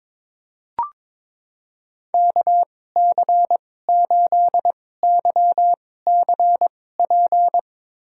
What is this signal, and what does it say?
Morse code, heard as a keyed tone, about 21 wpm, spaced as for 16 wpm, 700 Hz. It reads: KC8YCP
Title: USA Call Signs - 6 Characters - 2x Character Spacing - 22wpm